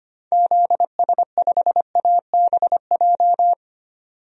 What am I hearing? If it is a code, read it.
Morse code, heard as a keyed tone, about 25 wpm, 700 Hz. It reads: ZS5ABJ
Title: DXCC-100 Call Signs - 6 Characters - 25wpm